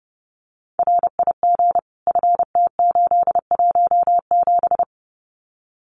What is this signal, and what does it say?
Morse code, heard as a keyed tone, about 30 wpm, 700 Hz. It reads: RIG FT817
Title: Common QSO Phrases Encore 30wpm